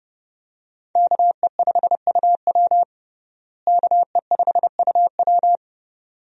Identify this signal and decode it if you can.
Morse code, heard as a keyed tone, about 30 wpm, 700 Hz. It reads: KE5UW KE5UW
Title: USA Call Signs - 5 Characters - Repeated 2x - 30wpm